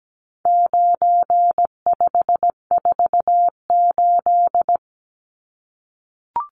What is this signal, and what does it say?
Morse code, heard as a keyed tone, about 17 wpm, 700 Hz. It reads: 9548
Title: Numbers - 4-Digit 17wpm